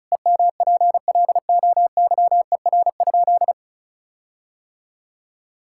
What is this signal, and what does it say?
Morse code, heard as a keyed tone, about 34 wpm, 700 Hz. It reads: EMPLOYER?
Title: Sentences from Top 1600 Words 35wpm